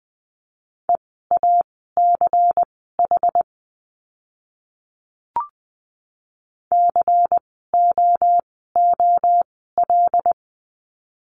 Morse code, heard as a keyed tone, about 20 wpm, spaced as for 14 wpm, 700 Hz. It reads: EACH COOL
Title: 094 Lesson 16f - Letter c - Words 20wpm